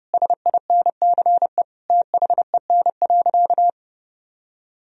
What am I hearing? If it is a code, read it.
Morse code, heard as a keyed tone, about 32 wpm, 700 Hz. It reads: SINCE THEN.
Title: Sentences from Top 1000 Words - Review Part 1 - 30wpm